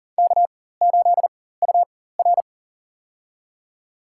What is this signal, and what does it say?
Morse code, heard as a keyed tone, about 40 wpm, spaced as for 19 wpm, 700 Hz. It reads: K8UR
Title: USA Call Signs - 4x Character Spacing - 40wpm